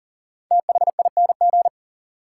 Morse code, heard as a keyed tone, about 38 wpm, 700 Hz. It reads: THING
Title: Sets of 3 Words - New Word Drill - 40wpm